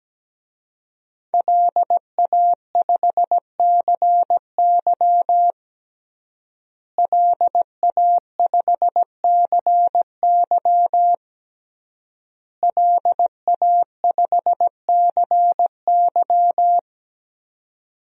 Morse code, heard as a keyed tone, about 17 wpm, 700 Hz. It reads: LA5CY LA5CY LA5CY
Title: DXCC-100 Call Signs - 5 Characters - Repeated 3x - 17wpm